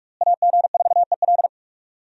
Morse code, heard as a keyed tone, about 46 wpm, 700 Hz. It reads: AG4EL
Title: USA Call Signs - 5 Characters - Repeated 3x - 45wpm